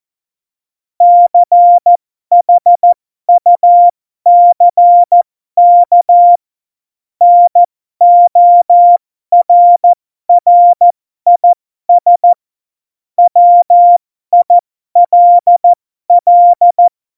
Morse code, heard as a keyed tone, about 14 wpm, spaced as for 12 wpm, 700 Hz. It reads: CHUCK NORRIS WILL
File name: Morsekurs / Witze